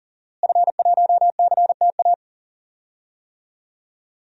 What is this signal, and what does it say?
Morse code, heard as a keyed tone, about 40 wpm, 700 Hz. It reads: F1CTA
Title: DXCC-100 Call Signs - 5 Characters - Repeated 2x - 40wpm